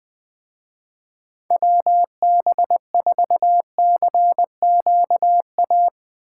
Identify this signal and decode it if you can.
Morse code, heard as a keyed tone, about 20 wpm, 700 Hz. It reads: WB4CQA
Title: Speed Racing - USA 2x3 Call Signs 17wpm